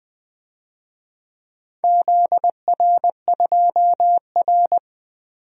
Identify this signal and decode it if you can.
Morse code, heard as a keyed tone, about 20 wpm, 700 Hz. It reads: ZR2R
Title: DXCC-100 Call Signs - 4 Characters - 20wpm